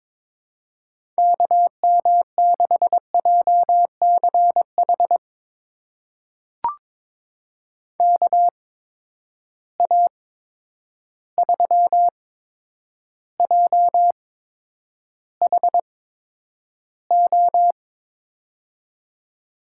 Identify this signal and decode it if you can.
Morse code, heard as a keyed tone, about 22 wpm, spaced as for 9 wpm, 700 Hz. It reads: KM6JCH KA3JHO
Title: USA Call Signs - 6 Characters - 8x Character Spacing - 22wpm